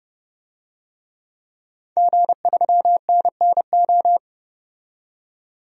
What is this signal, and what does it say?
Morse code, heard as a keyed tone, about 30 wpm, 700 Hz. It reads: G3NNO
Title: DXCC-100 Call Signs - 5 Characters - 30wpm